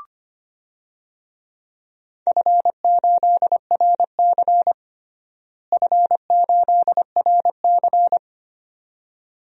Morse code, heard as a keyed tone, about 25 wpm, 700 Hz. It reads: F8RC F8RC
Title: DXCC-100 Call Signs - 4 Characters - Repeated 2x - 25wpm